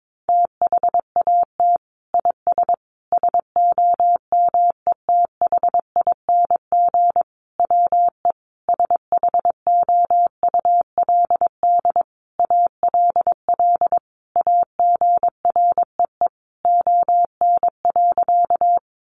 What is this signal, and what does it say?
Morse code, heard as a keyed tone, about 22 wpm, 700 Hz. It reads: THAT IS SOMETHING WE SHOULD ALL AGREE ON.
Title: Sentences from Top 1100 Words 22wpm